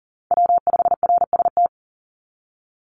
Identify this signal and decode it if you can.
Morse code, heard as a keyed tone, about 39 wpm, 700 Hz. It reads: W5RST
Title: NA SOTA Chaser Callsigns 40wpm